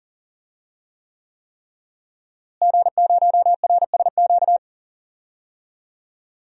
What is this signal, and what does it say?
Morse code, heard as a keyed tone, about 40 wpm, 700 Hz. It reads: G0RSQ
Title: DXCC-100 Call Signs - 5 Characters - Repeated 3x - 40wpm